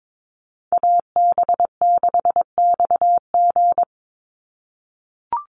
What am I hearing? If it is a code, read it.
Morse code, heard as a keyed tone, about 22 wpm, 700 Hz. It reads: AB6XG
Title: Call Signs - 22wpm